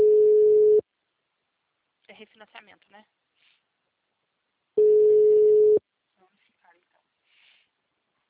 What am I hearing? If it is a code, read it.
Ringback tone (Europe, 425 Hz), with speech.
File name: out- -2016- - - .3632